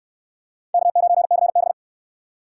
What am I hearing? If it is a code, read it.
Morse code, heard as a keyed tone, about 66 wpm, 700 Hz. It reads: K9CB